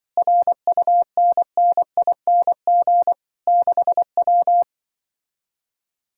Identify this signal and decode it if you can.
Morse code, heard as a keyed tone, about 24 wpm, 700 Hz. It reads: RUNNING 6W
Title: Speed-Racing Common QSO Phrases Encore - 20wpm